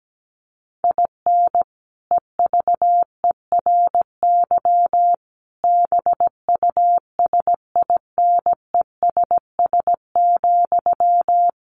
Morse code, heard as a keyed tone, about 17 wpm, 700 Hz. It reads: IN EVERY BUSINESS,
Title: Sentences from Top 1400 Words 17wpm